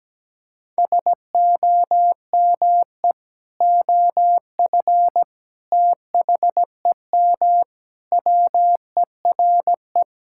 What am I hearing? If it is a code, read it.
Morse code, heard as a keyed tone, about 17 wpm, 700 Hz. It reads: SOME OF THEM WERE
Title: Sentences from Top 1000 Words - Review Part 6 - 17wpm